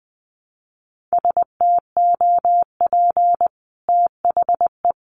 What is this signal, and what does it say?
Morse code, heard as a keyed tone, about 20 wpm, 700 Hz. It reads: STOP THE
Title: Sentences from Top 1200 Words 20wpm